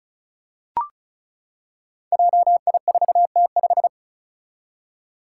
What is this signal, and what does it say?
Morse code, heard as a keyed tone, about 36 wpm, 700 Hz. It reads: JI4T5
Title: Five Challenging Letter-Number Combinations 35wpm